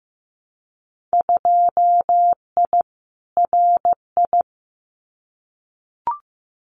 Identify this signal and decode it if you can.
Morse code, heard as a keyed tone, about 15 wpm, 700 Hz. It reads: 2I RI
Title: Winter Field Day Exchanges Encore 15wpm